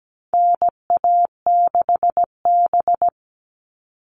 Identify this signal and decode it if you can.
Morse code, heard as a keyed tone, about 17 wpm, 700 Hz. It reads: NA6B
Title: All Cut Numbers - 4-Digit 17wpm